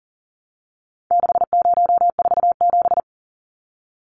Morse code, heard as a keyed tone, about 40 wpm, 700 Hz. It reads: 6047
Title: Numbers - 4-Digit 40wpm